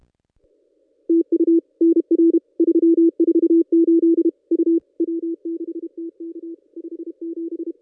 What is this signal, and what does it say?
Morse code, heard as a keyed tone, about 32 wpm, 330 Hz. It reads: TU NR 348 U W6TK 57
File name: contests / ss-cw-k5tr-op-n5rz -1800